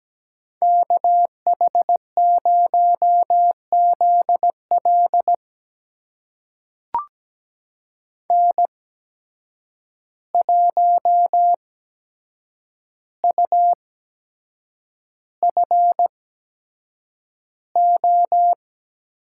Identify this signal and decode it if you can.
Morse code, heard as a keyed tone, about 17 wpm, spaced as for 7 wpm, 700 Hz. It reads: KH0ZL N1UFO
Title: USA Call Signs - 8x Character Spacing - 17wpm